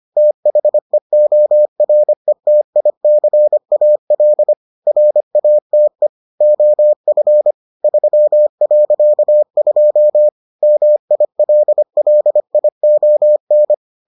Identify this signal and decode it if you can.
Morse code, heard as a keyed tone, about 25 wpm, 600 Hz. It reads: THEORETICAL RATE OF 3.2 MILLION